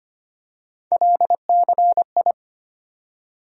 Morse code, heard as a keyed tone, about 25 wpm, 700 Hz. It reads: LCI
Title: USA Call Sign Postfixes - 3 Letters 25wpm